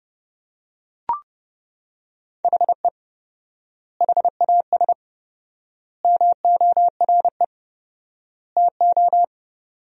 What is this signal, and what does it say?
Morse code, heard as a keyed tone, about 29 wpm, 700 Hz. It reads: HE HAS MORE TO
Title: 4x Word-Spacing - Sentences from Top 200 Words - 30wpm